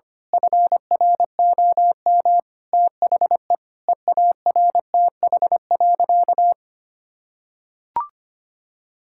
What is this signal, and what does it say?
Morse code, heard as a keyed tone, about 25 wpm, 700 Hz. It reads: FROM THE EARTH.
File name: practice / 25wpm / Sentences with 9-Word Phrases - 25wpm